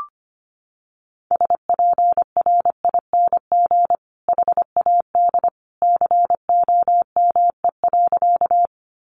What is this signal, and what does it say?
Morse code, heard as a keyed tone, about 25 wpm, 700 Hz. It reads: SPRING HAD COME.
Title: Sentences from Top 1200 Words 25wpm